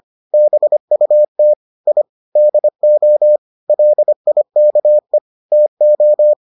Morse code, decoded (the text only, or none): BUT I DO LIKE TO